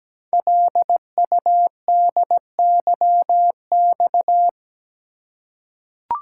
Morse code, heard as a keyed tone, about 17 wpm, 700 Hz. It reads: LUDYX
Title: Code Groups - 5 Letters - 17wpm